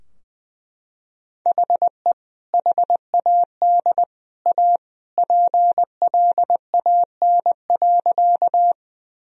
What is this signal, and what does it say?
Morse code, heard as a keyed tone, about 20 wpm, 700 Hz. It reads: HE HAD A PLAN.
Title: Sentences from Top 1000 Words - Review Part 2 - 20wpm